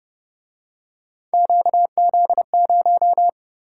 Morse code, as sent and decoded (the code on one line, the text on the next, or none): --.- --.. -----
QZ0